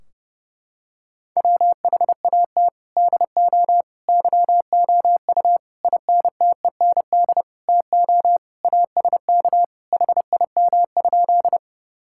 Morse code, heard as a keyed tone, about 30 wpm, 700 Hz. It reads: WHAT DO YOU INTEND TO ASK HIM?
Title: Sentences from Top 1600 Words 30wpm